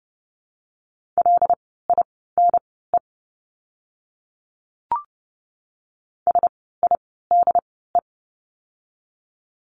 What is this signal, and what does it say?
Morse code, heard as a keyed tone, about 30 wpm, spaced as for 17 wpm, 700 Hz. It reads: LINE SIDE
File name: course / 30:10wpm / 115 Lesson 19f - Letter w - Words 30wpm